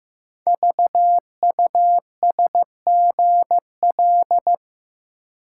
Morse code, heard as a keyed tone, about 15 wpm, 700 Hz. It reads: VUSGL
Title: Code Groups - 5 Letters - 15wpm